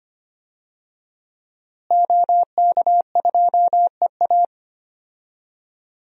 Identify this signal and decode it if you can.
Morse code, heard as a keyed tone, about 25 wpm, 700 Hz. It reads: OK2EA
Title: DXCC-100 Call Signs - 5 Characters - Repeated 2x - 25wpm